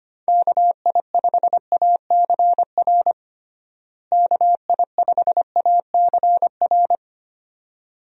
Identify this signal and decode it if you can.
Morse code, heard as a keyed tone, about 25 wpm, 700 Hz. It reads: KI5ACR KI5ACR
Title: USA Call Signs - 6 Characters - Repeated 2x - 25wpm